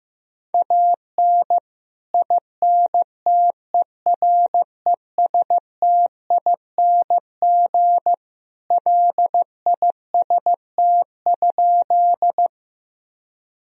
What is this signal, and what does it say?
Morse code, heard as a keyed tone, about 15 wpm, 700 Hz. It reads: AN INTERESTING LIST?
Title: Sentences from Top 1000 Words - Review Part 2 - 15wpm